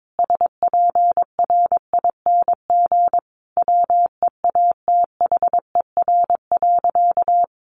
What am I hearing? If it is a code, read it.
Morse code, heard as a keyed tone, about 22 wpm, 700 Hz. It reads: SPRING WEATHER.
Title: Sentences from Top 1200 Words 22wpm